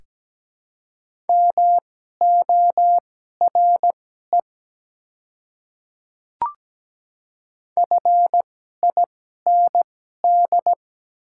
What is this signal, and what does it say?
Morse code, heard as a keyed tone, about 17 wpm, spaced as for 12 wpm, 700 Hz. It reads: MORE FIND
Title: 2x Character Spacing - Top 300 Words - 17wpm